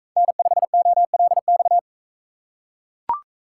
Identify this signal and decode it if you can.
Morse code, heard as a keyed tone, about 40 wpm, 700 Hz. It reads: N5OLX